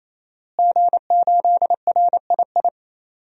Morse code, heard as a keyed tone, about 28 wpm, 700 Hz. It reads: G8RII
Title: DXCC-100 Call Signs - 5 Characters - Repeated 3x - 28wpm